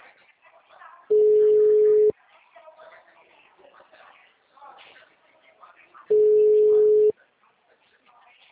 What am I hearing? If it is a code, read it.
Ringback tone (Europe, 425 Hz), with speech.